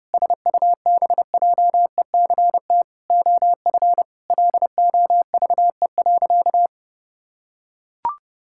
Morse code, heard as a keyed tone, about 32 wpm, 700 Hz. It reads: SUBJECT OF LOVE.
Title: Sentences from Top 1600 Words 30wpm